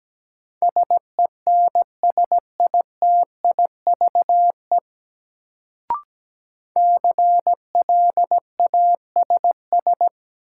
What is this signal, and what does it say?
Morse code, heard as a keyed tone, about 17 wpm, 700 Hz. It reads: SENSITIVE CLASS
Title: Sentences from Top 1400 Words - New Word Drill Alt - 17wpm